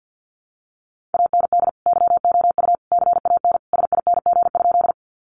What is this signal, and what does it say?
Morse code, heard as a keyed tone, about 50 wpm, 700 Hz. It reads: AND YOU CAN SING?
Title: Sentences from Top 400 Words 50wpm